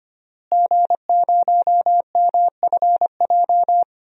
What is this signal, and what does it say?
Morse code, heard as a keyed tone, about 25 wpm, 700 Hz. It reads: G0MFJ
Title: DXCC-100 Call Signs - 5 Characters - Repeated 2x - 25wpm